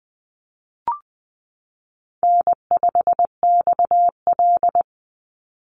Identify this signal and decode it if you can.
Morse code, heard as a keyed tone, about 20 wpm, 700 Hz. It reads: N5XL
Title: NA SOTA Chaser Callsigns 20wpm